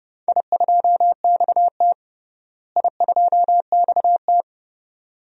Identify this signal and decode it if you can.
Morse code, heard as a keyed tone, about 32 wpm, 700 Hz. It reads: I2XT I2XT
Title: DXCC-100 Call Signs - 4 Characters - Repeated 2x - 30wpm